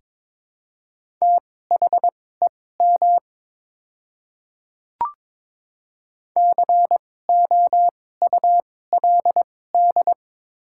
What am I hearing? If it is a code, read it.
Morse code, heard as a keyed tone, about 23 wpm, spaced as for 16 wpm, 700 Hz. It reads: THEM COULD
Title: 2x Character Spacing - Top 300 Words - 22wpm